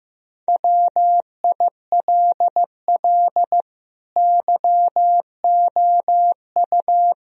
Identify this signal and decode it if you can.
Morse code, heard as a keyed tone, about 15 wpm, 700 Hz. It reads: WILL YOU
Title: Sentences from Top 1000 Words - Review Part 4 - 15wpm